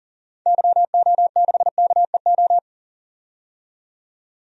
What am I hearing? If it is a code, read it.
Morse code, heard as a keyed tone, about 40 wpm, 700 Hz. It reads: YO6KEO
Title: DXCC-100 Call Signs - 6 Characters - 40wpm